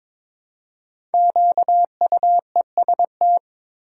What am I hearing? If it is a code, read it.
Morse code, heard as a keyed tone, about 22 wpm, 700 Hz. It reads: QUEST
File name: 5 Letter Words - First 3 Letters Given - 22wpm